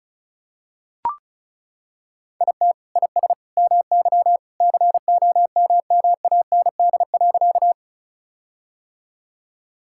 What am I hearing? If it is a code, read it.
Morse code, heard as a keyed tone, about 36 wpm, 700 Hz. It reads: IT IS MY COMMAND.
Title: Sentences from Top 1600 Words 35wpm